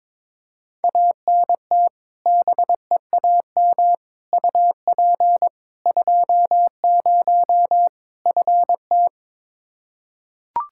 Morse code, heard as a keyed tone, about 22 wpm, 700 Hz. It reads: ANT BEAM UP 20 FT
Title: K1USN SST QSO Practice 22wpm